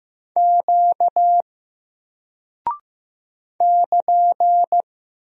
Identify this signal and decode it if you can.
Morse code, heard as a keyed tone, about 15 wpm, 700 Hz. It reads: Q <KN>
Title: Single Character (Letter:Number:Punctuation:Prosign) 15wpm